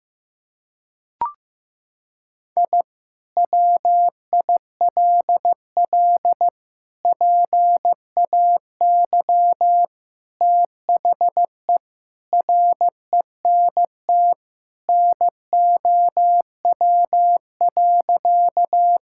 Morse code, heard as a keyed tone, about 15 wpm, 700 Hz. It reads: I WILL PAY THE RENT NOW.
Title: Sentences from Top 1000 Words - Review Part 5 - 15wpm